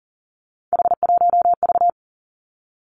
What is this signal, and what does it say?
Morse code, heard as a keyed tone, about 40 wpm, 700 Hz. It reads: H1V
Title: Three Challenging Letter-Number Combinations 40wpm